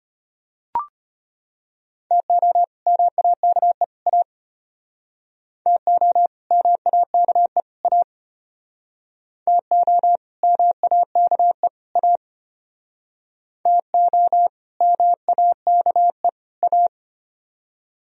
Morse code, heard as a keyed tone, about 32 wpm, 700 Hz. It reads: TO MAKE A TO MAKE A TO MAKE A TO MAKE A